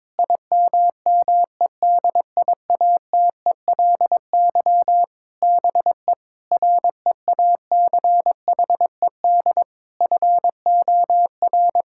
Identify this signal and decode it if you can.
Morse code, heard as a keyed tone, about 22 wpm, 700 Hz. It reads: IMMEDIATELY BE REACHED FOR
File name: Sentences with 7-Word Phrases - 22wpm